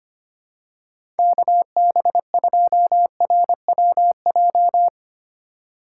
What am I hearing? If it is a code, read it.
Morse code, heard as a keyed tone, about 25 wpm, 700 Hz. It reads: KB2RWJ
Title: USA Call Signs - 6 Characters - 2x Character Spacing - 25wpm